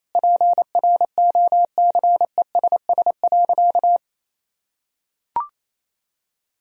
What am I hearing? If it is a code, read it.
Morse code, heard as a keyed tone, about 28 wpm, 700 Hz. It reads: PROCESS.
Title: Sentences from Top 1000 Words - Review Part 3 - 28wpm